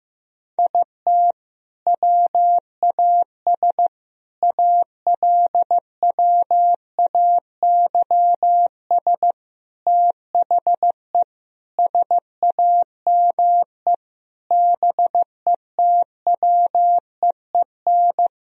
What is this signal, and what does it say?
Morse code, heard as a keyed tone, about 15 wpm, 700 Hz. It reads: IT WAS ALWAYS THE SAME BETWEEN
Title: Sentences from Top 1000 Words - Review Part 1 - 15wpm